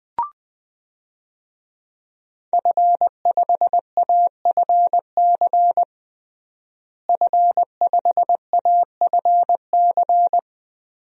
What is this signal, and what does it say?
Morse code, heard as a keyed tone, about 20 wpm, 700 Hz. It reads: F5AFC F5AFC
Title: DXCC-100 Call Signs - 5 Characters - Repeated 3x - 20wpm